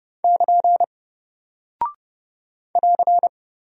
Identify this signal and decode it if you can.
Morse code, heard as a keyed tone, about 30 wpm, 700 Hz. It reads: <KN> <AR>
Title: Single Character (Letter:Number:Punctuation:Prosign) 30wpm